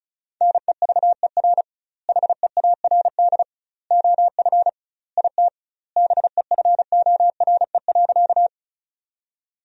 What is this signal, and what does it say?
Morse code, heard as a keyed tone, about 36 wpm, 700 Hz. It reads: NEVER HEARD OF IT BEFORE.
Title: 2x Word-Spacing - Sentences from Top 400 Words - 35wpm